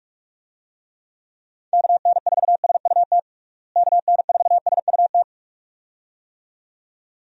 Morse code, heard as a keyed tone, about 47 wpm, 700 Hz. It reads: KN4SUT KN4SUT
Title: USA Call Signs - 6 Characters - Repeated 2x - 45wpm